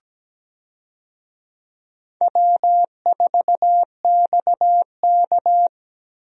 Morse code, heard as a keyed tone, about 17 wpm, 700 Hz. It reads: W4XK
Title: USA Call Signs - 4 Characters - Repeated 3x - 17wpm